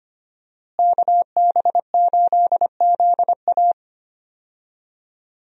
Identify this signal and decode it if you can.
Morse code, heard as a keyed tone, about 25 wpm, 700 Hz. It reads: KB8ZA
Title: USA Call Signs - 5 Characters - Repeated 3x - 25wpm